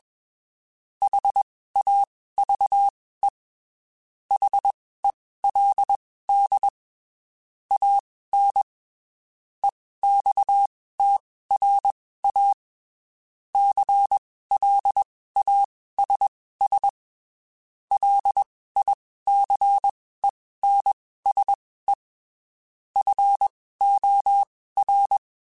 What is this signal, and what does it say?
Morse code, heard as a keyed tone, about 21 wpm, spaced as for 15 wpm, 800 Hz. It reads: HAVE HELD AN EXTRA CLASS LICENSE FOR
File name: (spaced) / QSOs Spaced-mp3s / WA7QOX 2 @20 sp